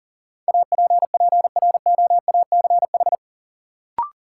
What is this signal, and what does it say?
Morse code, heard as a keyed tone, about 40 wpm, 700 Hz. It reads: APPROACH
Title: Sentences from Top 1400 Words - New Word Drill Alt - 40wpm